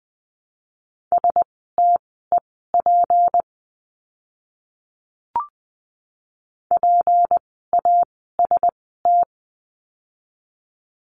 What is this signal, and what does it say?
Morse code, heard as a keyed tone, about 20 wpm, spaced as for 14 wpm, 700 Hz. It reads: STEP PAST
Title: 153 Lesson 25f - Letter p - Words 20wpm